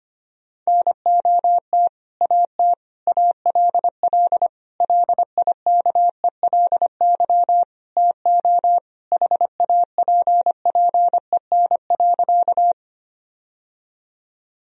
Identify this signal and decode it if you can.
Morse code, heard as a keyed tone, about 25 wpm, 700 Hz. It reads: NOT AT ALL LIKELY TO HAPPEN.
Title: Sentences from Top 1000 Words - Review Part 4 - 25wpm